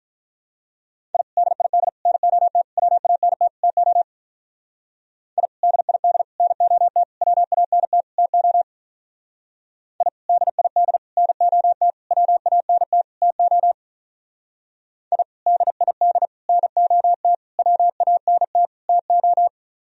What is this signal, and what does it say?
Morse code, heard as a keyed tone, about 43 wpm, 700 Hz. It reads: I DID NOT WANT TO I DID NOT WANT TO I DID NOT WANT TO I DID NOT WANT TO
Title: Speed-Racing - Sets of 5 Words 35wpm